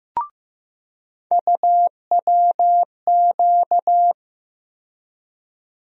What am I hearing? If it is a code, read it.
Morse code, heard as a keyed tone, about 15 wpm, 700 Hz. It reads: UWQ
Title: USA Call Sign Postfixes - 3 Letters 15wpm